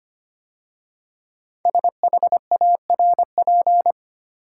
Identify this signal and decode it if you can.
Morse code, heard as a keyed tone, about 25 wpm, 700 Hz. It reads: SHARP
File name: 5 Letter Words - First 3 Letters Given - 25wpm